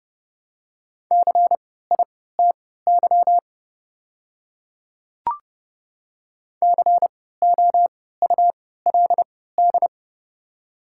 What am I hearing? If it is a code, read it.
Morse code, heard as a keyed tone, about 30 wpm, spaced as for 17 wpm, 700 Hz. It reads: CITY COULD
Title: 3x Character Spacing - Top 300 Words - 30wpm